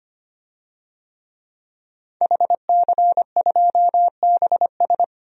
Morse code, heard as a keyed tone, about 25 wpm, 690 Hz. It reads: HC2BS
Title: DXCC-100 Call Signs - 5 Characters - 25wpm